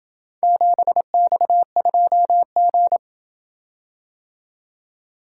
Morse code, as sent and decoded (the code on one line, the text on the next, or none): --... -..- ..--- --.
7X2G